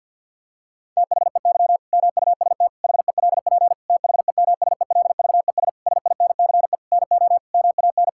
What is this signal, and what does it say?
Morse code, heard as a keyed tone, about 50 wpm, 700 Hz. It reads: THEY MUST HELP THEMSELVES SINCE NO MAN
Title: Sentences from Top 1000 Words - Review Part 5 - 50wpm